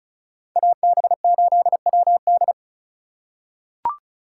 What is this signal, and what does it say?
Morse code, heard as a keyed tone, about 34 wpm, 700 Hz. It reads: AB8WD
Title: USA Call Signs - 6x Character Spacing - 35wpm